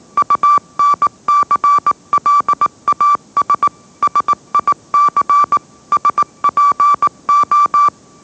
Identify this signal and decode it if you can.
Morse code, heard as a keyed tone, about 18 wpm, 1160 Hz. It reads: UNCLAS SIC SPO
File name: mrx / mrx081 20wpm